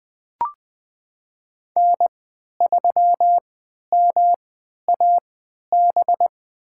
Morse code, heard as a keyed tone, about 20 wpm, spaced as for 11 wpm, 700 Hz. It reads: N3MAB
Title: USA Call Signs - 5 Characters - 3x Character Spacing - 20wpm